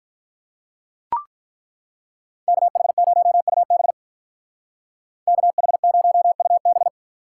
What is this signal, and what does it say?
Morse code, heard as a keyed tone, about 43 wpm, 700 Hz. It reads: KH0UB KH0UB
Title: Speed Racing - USA 2x2 Call Signs 35wpm